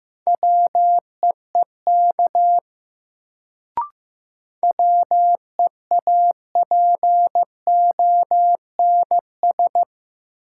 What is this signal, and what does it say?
Morse code, heard as a keyed tone, about 15 wpm, 700 Hz. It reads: WEEK WEAPONS
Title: Sets of 5 Words Encore - New Word Drill - 15wpm